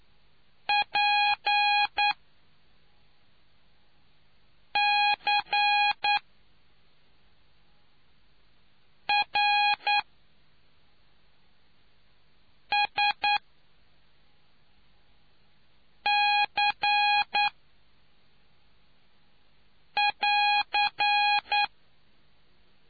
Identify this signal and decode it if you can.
Morse code, heard as a keyed tone, about 9 wpm, spaced as for 3 wpm, 830 Hz. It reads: PCRSC<AR>